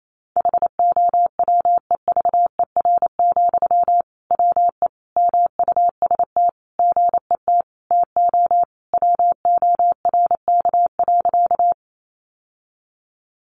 Morse code, heard as a keyed tone, about 28 wpm, 700 Hz. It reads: HOWEVER, WE MUST GET TO WORK.
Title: Sentences from Top 600 Words 28wpm